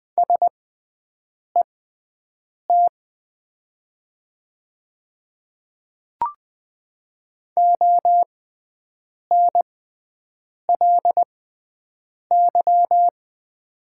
Morse code, heard as a keyed tone, about 20 wpm, spaced as for 7 wpm, 700 Hz. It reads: SET ONLY
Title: 6x Character Spacing - Top 200 Words - 20wpm